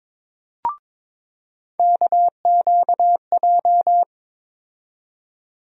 Morse code, heard as a keyed tone, about 22 wpm, 700 Hz. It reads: KQJ
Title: USA Call Sign Postfixes - 3 Letters 22wpm